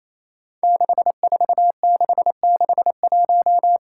Morse code, heard as a keyed tone, about 28 wpm, 700 Hz. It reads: 64661
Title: Five Challenging Letter-Number Combinations 28wpm